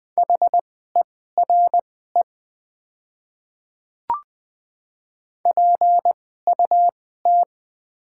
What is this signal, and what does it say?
Morse code, heard as a keyed tone, about 20 wpm, spaced as for 14 wpm, 700 Hz. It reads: HERE PUT
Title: 2x Character Spacing - Top 200 Words - 20wpm